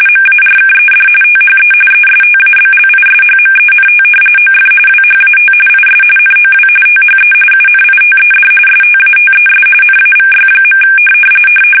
RTTY, about 45.45 baud. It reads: MARKED THE ESTABLISHMENT OF WHAT IS NOW THE UNITED STATES NAVY.
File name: fsk850 / usnavy1